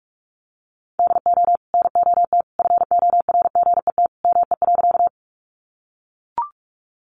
Morse code, heard as a keyed tone, about 44 wpm, 700 Hz. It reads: DO NOT FORGET ME.
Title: Sentences from Top 1200 Words 45wpm